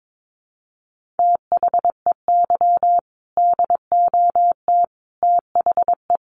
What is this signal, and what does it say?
Morse code, heard as a keyed tone, about 22 wpm, 700 Hz. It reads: THEY DOT THE
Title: Sentences from Top 1500 Words 22wpm